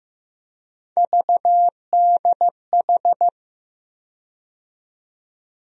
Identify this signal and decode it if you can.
Morse code, heard as a keyed tone, about 16 wpm, 700 Hz. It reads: VDH
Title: USA Call Sign Postfixes - 3 Letters 15wpm